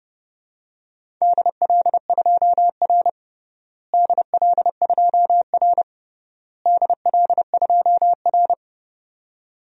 Morse code, heard as a keyed tone, about 30 wpm, 700 Hz. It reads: DL2R DL2R DL2R
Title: DXCC-100 Call Signs - 4 Characters - Repeated 3x - 30wpm